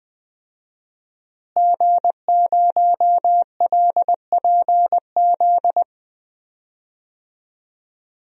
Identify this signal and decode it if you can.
Morse code, heard as a keyed tone, about 20 wpm, 700 Hz. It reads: G0LPZ
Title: DXCC-100 Call Signs - 5 Characters - Repeated 3x - 20wpm